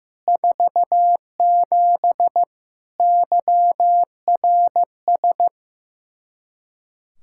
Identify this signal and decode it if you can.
Morse code, heard as a keyed tone, about 16 wpm, 700 Hz. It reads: 47 YRS
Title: K1USN SST QSO Practice 15wpm